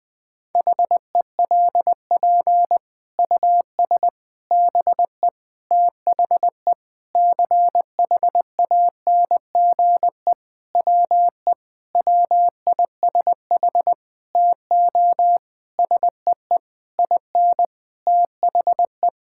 Morse code, heard as a keyed tone, about 20 wpm, 700 Hz. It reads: HELP US BE THE CHANGE WE WISH TO SEE IN THE
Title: Sentences with 9-Word Phrases - Spoken Beginning - 20wpm